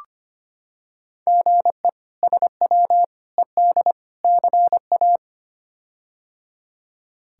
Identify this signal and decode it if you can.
Morse code, heard as a keyed tone, about 25 wpm, 700 Hz. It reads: GE SW ED CA
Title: K1USN SST QSO Practice - 25wpm